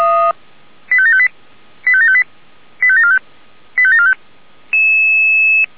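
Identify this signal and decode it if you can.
Five-tone selective call (ZVEI), heard as quick runs of five tones, tones of about 70 ms. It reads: 85758 85758 85745 85745
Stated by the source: ZVEI five-tone selective call 85730, received by SDR.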